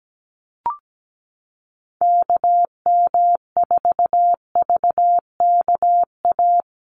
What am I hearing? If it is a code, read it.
Morse code, heard as a keyed tone, about 17 wpm, 700 Hz. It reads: KM4VKA
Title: USA Call Signs 17wpm